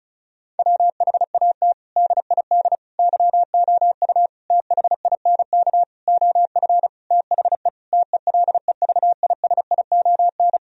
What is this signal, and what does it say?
Morse code, heard as a keyed tone, about 36 wpm, 700 Hz. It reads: WHAT DID YOU THINK OF THE TELEVISION
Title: Sentences from Top 1000 Words - Review Part 4 - 35wpm